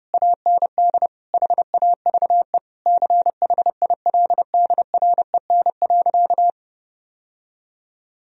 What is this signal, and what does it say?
Morse code, heard as a keyed tone, about 32 wpm, 700 Hz. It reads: AND HAVE CHILDREN.
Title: Sentences from Top 1000 Words - Review Part 1 - 30wpm